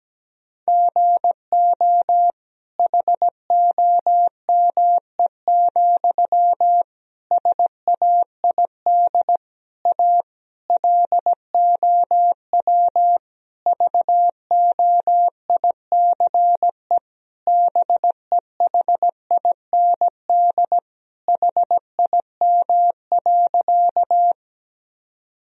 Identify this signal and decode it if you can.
Morse code, heard as a keyed tone, about 17 wpm, 700 Hz. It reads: GO HOME, SAID A LOW VOICE BEHIND HIM.
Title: Sentences from Top 1000 Words - Review Part 1 - 17wpm